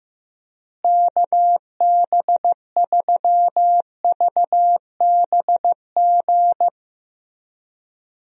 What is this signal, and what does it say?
Morse code, heard as a keyed tone, about 15 wpm, 700 Hz. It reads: KB3VBG
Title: USA Call Signs - 6 Characters - Repeated 3x - 15wpm